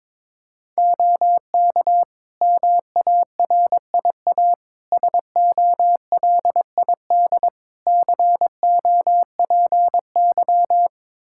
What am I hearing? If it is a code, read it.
Morse code, heard as a keyed tone, about 22 wpm, 700 Hz. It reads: OK MARIA SOLID COPY
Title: Practice QSO - Exchange 3 - More Info - 22wpm